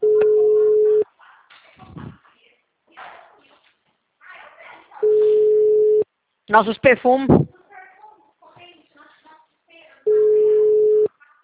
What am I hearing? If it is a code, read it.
Ringback tone (Europe, 425 Hz), with speech.